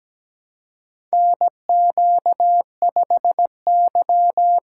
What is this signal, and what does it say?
Morse code, heard as a keyed tone, about 17 wpm, 700 Hz. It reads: NQ5Y
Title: USA Call Signs - 4 Characters - 2x Character Spacing - 17wpm